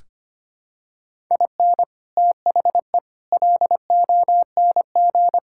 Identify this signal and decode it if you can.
Morse code, heard as a keyed tone, about 25 wpm, 700 Hz. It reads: IN THE LONG
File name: practice / 25wpm / Sentences with 4-Word Phrases - 25wpm